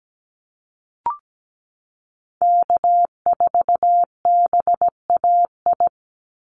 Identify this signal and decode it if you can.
Morse code, heard as a keyed tone, about 17 wpm, 700 Hz. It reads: K4BAI
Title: Call Signs Repeated 3x - 17wpm